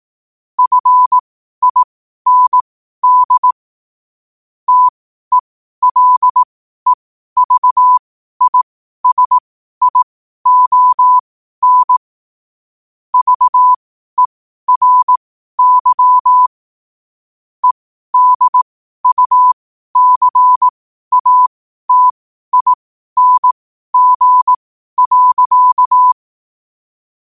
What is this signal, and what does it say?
Morse code, heard as a keyed tone, about 18 wpm, spaced as for 12 wpm, 1000 Hz. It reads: FIND TELEVISION VERY EDUCATING.
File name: qotdfiles / 12wpm